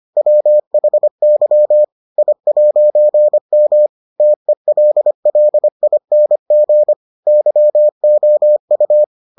Morse code, heard as a keyed tone, about 25 wpm, 600 Hz. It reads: WHY I'M TELLING YOU